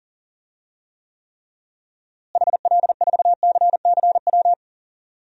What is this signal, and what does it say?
Morse code, heard as a keyed tone, about 37 wpm, 700 Hz. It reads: HL4CCW
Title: DXCC-100 Call Signs - 6 Characters - Repeated 3x - 40wpm